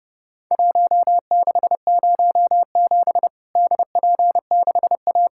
Morse code, heard as a keyed tone, about 29 wpm, 700 Hz. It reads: 1607 DP6A
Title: Worked All Europe DX Contest - QTC Exchange - 30wpm